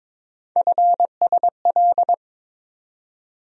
Morse code, heard as a keyed tone, about 22 wpm, 700 Hz. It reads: FSL
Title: USA Call Sign Postfixes - 3 Letters 22wpm